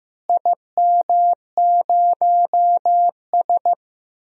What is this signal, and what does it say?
Morse code, heard as a keyed tone, about 15 wpm, 700 Hz. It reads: IM0S
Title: DXCC-100 Call Signs - 4 Characters - Repeated 2x - 15wpm